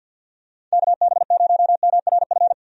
Morse code, heard as a keyed tone, about 48 wpm, 700 Hz. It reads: KB0MRF
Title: USA Call Signs - 6 Characters - 50wpm